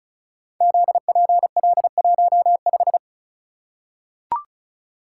Morse code, heard as a keyed tone, about 36 wpm, 700 Hz. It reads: ZPL15
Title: Code Groups - 5 Letter:Numbers - 35wpm